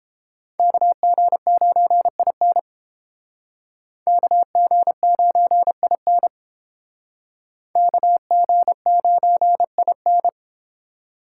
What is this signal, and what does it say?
Morse code, heard as a keyed tone, about 29 wpm, 700 Hz. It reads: KG9IN KG9IN KG9IN